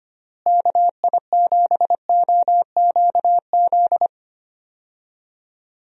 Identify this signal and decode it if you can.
Morse code, heard as a keyed tone, about 25 wpm, 700 Hz. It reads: KI7OQZ
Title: USA Call Signs - 6 Characters - Repeated 3x - 25wpm